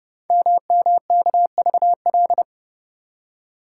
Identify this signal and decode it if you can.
Morse code, heard as a keyed tone, about 30 wpm, 700 Hz. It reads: MMKVL
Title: Code Groups - 5 Letters - 30wpm